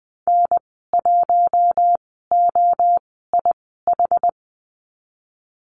Morse code, heard as a keyed tone, about 20 wpm, spaced as for 15 wpm, 700 Hz. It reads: N1OIH
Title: 062 Lesson 11g - Letter h - Call Signs 20wpm